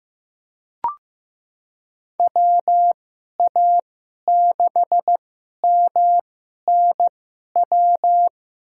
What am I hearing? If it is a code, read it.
Morse code, heard as a keyed tone, about 16 wpm, spaced as for 11 wpm, 700 Hz. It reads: WA6MNW
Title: USA Call Signs - 6 Characters - 2x Character Spacing - 15wpm